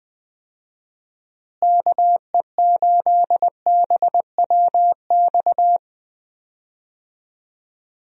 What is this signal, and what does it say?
Morse code, heard as a keyed tone, about 21 wpm, 700 Hz. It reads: KE8BWX
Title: USA Call Signs - 6 Characters - Repeated 3x - 20wpm